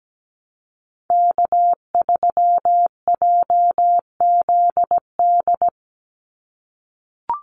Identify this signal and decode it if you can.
Morse code, heard as a keyed tone, about 17 wpm, 700 Hz. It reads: K3JZD
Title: NA SOTA Chaser Callsigns 17wpm